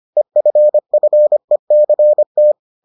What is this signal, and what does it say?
Morse code, heard as a keyed tone, about 25 wpm, 600 Hz. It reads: EFFECT